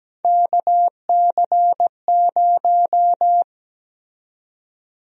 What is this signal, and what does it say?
Morse code, heard as a keyed tone, about 17 wpm, 700 Hz. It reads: KC0
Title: USA Call Sign Prefixes - Repeated 2x - 17wpm